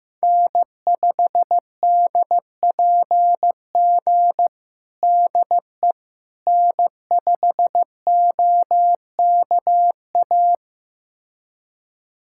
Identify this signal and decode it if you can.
Morse code, heard as a keyed tone, about 16 wpm, 700 Hz. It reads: N5DPG DE N5OKA
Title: USA Callsign DE Callsign - 5 Characters - 15wpm